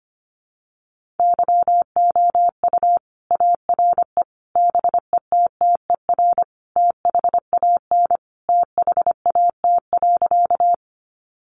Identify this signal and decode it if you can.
Morse code, heard as a keyed tone, about 25 wpm, 700 Hz. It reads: YOU ARE BETTER THAN THAT.
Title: Sentences from Top 400 Words 25wpm